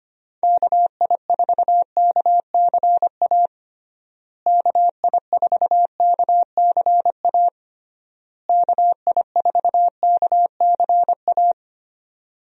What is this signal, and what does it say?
Morse code, heard as a keyed tone, about 25 wpm, 700 Hz. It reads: KI4KCA KI4KCA KI4KCA
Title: USA Call Signs - 6 Characters - Repeated 3x - 25wpm